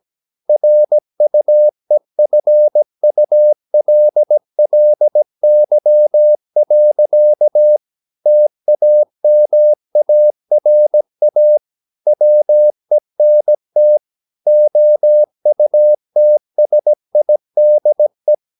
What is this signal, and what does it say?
Morse code, heard as a keyed tone, about 17 wpm, 600 Hz. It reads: RUEFULLY. TAMARA WENT OUTSIDE
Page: Romanen Never af Ken Follet på engelsk.